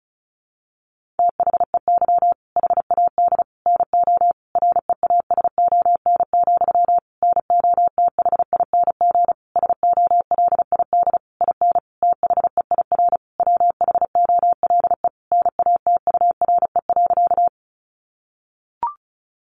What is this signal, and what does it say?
Morse code, heard as a keyed tone, about 34 wpm, 700 Hz. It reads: THEY HAD NO REASON, NOTHING SOLID IN THEIR WHOLE NATURE.
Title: Sentences from Top 1500 Words 35wpm